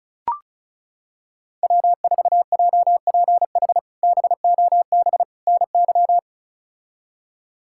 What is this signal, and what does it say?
Morse code, heard as a keyed tone, about 36 wpm, 700 Hz. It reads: W4JPH BOB NY
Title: K1USN SST QSO Practice - 35wpm